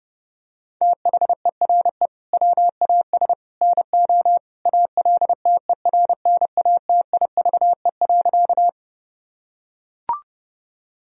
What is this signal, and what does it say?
Morse code, heard as a keyed tone, about 30 wpm, 700 Hz. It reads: THERE WAS NO ALTERNATIVE.
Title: Sentences from Top 1000 Words - Review Part 4 - 30wpm